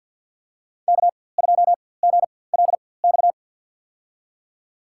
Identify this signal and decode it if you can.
Morse code, heard as a keyed tone, about 50 wpm, spaced as for 23 wpm, 700 Hz. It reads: K2GLX
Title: USA Call Signs - 5 Characters - 4x Character Spacing - 50wpm